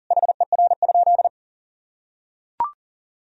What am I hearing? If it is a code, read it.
Morse code, heard as a keyed tone, about 40 wpm, 700 Hz. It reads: HER?
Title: Sentences from Top 1000 Words - Review Part 4 - 40wpm